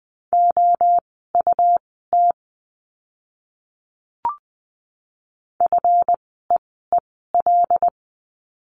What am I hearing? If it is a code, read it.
Morse code, heard as a keyed tone, about 20 wpm, spaced as for 15 wpm, 700 Hz. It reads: OUT FEEL
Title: 146 Lesson 24f - Letter y - Words 20wpm